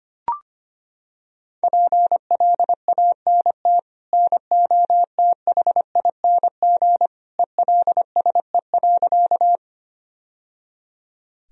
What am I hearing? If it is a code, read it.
Morse code, heard as a keyed tone, about 25 wpm, 700 Hz. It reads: PLANT NOTHING ELSE.
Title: Sentences from Top 1000 Words - Review Part 1 - 25wpm